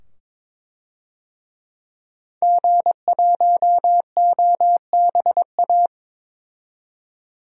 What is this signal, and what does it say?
Morse code, heard as a keyed tone, about 21 wpm, 700 Hz. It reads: G1OBA
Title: DXCC-100 Call Signs - 5 Characters - 22wpm